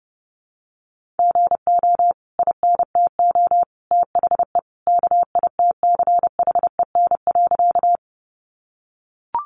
Morse code, heard as a keyed tone, about 30 wpm, 700 Hz. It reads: GO INTO THE KITCHEN.
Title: Sentences from Top 1400 Words 30wpm